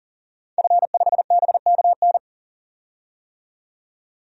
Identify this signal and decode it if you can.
Morse code, heard as a keyed tone, about 40 wpm, 700 Hz. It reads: F5BKN
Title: DXCC-100 Call Signs - 5 Characters - 40wpm